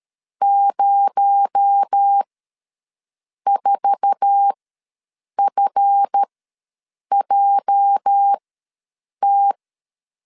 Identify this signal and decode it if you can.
Morse code, heard as a keyed tone, about 13 wpm, spaced as for 7 wpm, 800 Hz. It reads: 04FJT
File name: LEZIONI / 10 SIGNS / 10C DOT ws 12wpm 3548